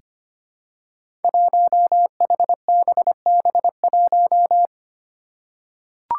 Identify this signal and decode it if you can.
Morse code, heard as a keyed tone, about 25 wpm, 700 Hz. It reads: 1HBB1
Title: Five Challenging Letter-Number Combinations 25wpm